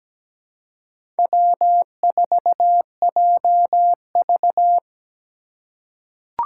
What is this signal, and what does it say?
Morse code, heard as a keyed tone, about 17 wpm, 700 Hz. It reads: W4JV
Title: USA Call Signs - 4x Character Spacing - 17wpm